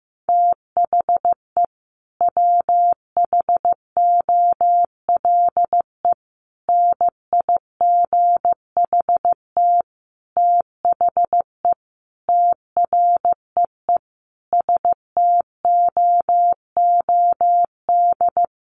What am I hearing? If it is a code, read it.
Morse code, heard as a keyed tone, about 15 wpm, 700 Hz. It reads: THE WHOLE NIGHT THE TREE STOOD
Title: Sentences from Top 500 Words 15wpm